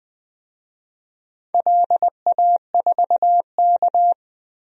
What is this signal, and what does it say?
Morse code, heard as a keyed tone, about 20 wpm, 700 Hz. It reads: LA4K
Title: DXCC-100 Call Signs - 4 Characters - 20wpm